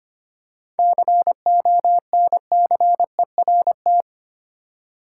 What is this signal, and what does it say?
Morse code, heard as a keyed tone, about 25 wpm, 700 Hz. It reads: CONCERT
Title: Seven Letter Words - 25wpm